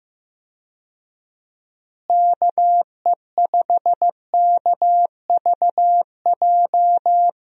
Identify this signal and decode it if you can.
Morse code, heard as a keyed tone, about 15 wpm, 700 Hz. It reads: KE5KVJ
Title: USA Call Signs - 6 Characters - Repeated 3x - 15wpm